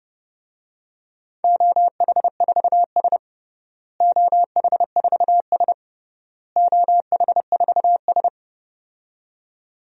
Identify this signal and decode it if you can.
Morse code, heard as a keyed tone, about 30 wpm, 700 Hz. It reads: OH4S OH4S OH4S
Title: DXCC-100 Call Signs - 4 Characters - Repeated 3x - 30wpm